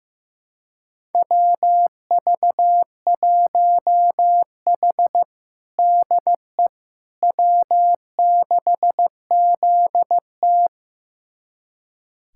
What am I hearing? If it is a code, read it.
Morse code, heard as a keyed tone, about 15 wpm, 700 Hz. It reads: WV1H DE W6ZT
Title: USA Callsign DE Callsign - 4 Characters - 15wpm